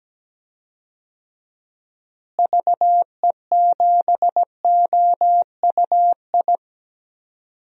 Morse code, heard as a keyed tone, about 17 wpm, 700 Hz. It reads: VE7OUI
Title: DXCC-100 Call Signs - 6 Characters - Repeated 3x - 17wpm